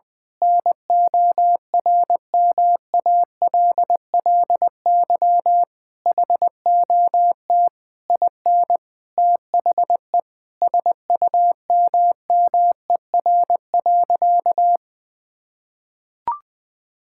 Morse code, heard as a keyed tone, about 20 wpm, 700 Hz. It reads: NORMALLY HOT IN THE SUMMER.
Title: Sentences from Top 1000 Words - Review Part 6 - 20wpm